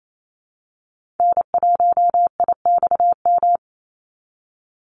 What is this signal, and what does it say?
Morse code, heard as a keyed tone, about 28 wpm, 700 Hz. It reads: N1IXM
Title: USA Call Signs 28wpm